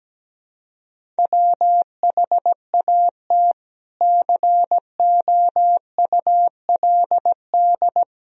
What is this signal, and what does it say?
Morse code, heard as a keyed tone, about 17 wpm, 700 Hz. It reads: WHAT COULD
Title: Sentences from Top 400 Words and Abbreviations 17wpm